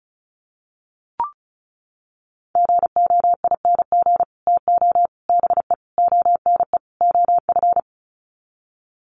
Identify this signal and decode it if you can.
Morse code, heard as a keyed tone, about 34 wpm, 700 Hz. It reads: GOING TO BE ONE OF
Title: Sets of 5 Words Encore2 35wpm